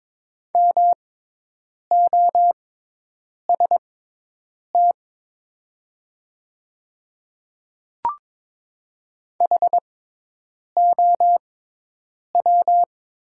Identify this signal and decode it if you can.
Morse code, heard as a keyed tone, about 22 wpm, spaced as for 8 wpm, 690 Hz. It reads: MOST HOW
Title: 6x Character Spacing - Top 100 Words - 22wpm